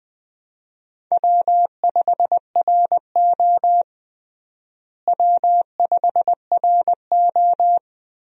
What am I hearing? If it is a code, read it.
Morse code, heard as a keyed tone, about 20 wpm, 700 Hz. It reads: W5RO W5RO
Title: USA Call Signs - 4 Characters - Repeated 2x - 20wpm